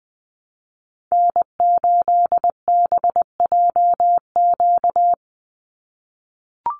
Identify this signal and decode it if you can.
Morse code, heard as a keyed tone, about 20 wpm, 700 Hz. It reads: N8BJQ
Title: Call Signs - 20wpm